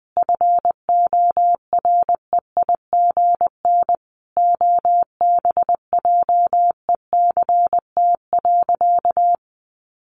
Morse code, heard as a keyed tone, about 20 wpm, 700 Hz. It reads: FOREIGN OBJECT.
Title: Sentences from Top 1400 Words 20wpm